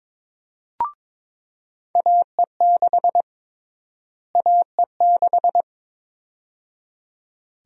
Morse code, heard as a keyed tone, about 22 wpm, 700 Hz. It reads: AE6 AE6
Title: USA Call Sign Prefixes - Repeated 2x - 22wpm